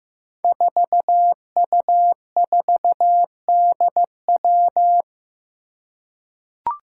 Morse code, heard as a keyed tone, about 15 wpm, 700 Hz. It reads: 4U4DW
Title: Five Challenging Letter-Number Combinations 15wpm